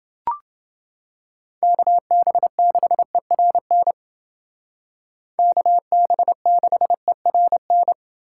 Morse code, heard as a keyed tone, about 28 wpm, 700 Hz. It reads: KB6ERN KB6ERN